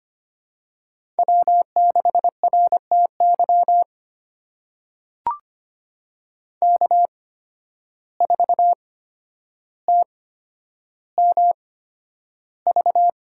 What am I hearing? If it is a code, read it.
Morse code, heard as a keyed tone, about 25 wpm, spaced as for 11 wpm, 700 Hz. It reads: W6RTY K4TMV
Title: USA Call Signs - 5 Characters - 8x Character Spacing - 25wpm